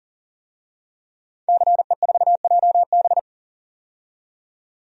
Morse code, heard as a keyed tone, about 40 wpm, 700 Hz. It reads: CE4JB
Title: DXCC-100 Call Signs - 5 Characters - 40wpm